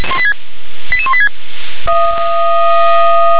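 Five-tone selective call (ZVEI), heard as quick runs of five tones, tones of about 70 ms. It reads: 88186
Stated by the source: ZVEI five-tone call 88164 received by SDR.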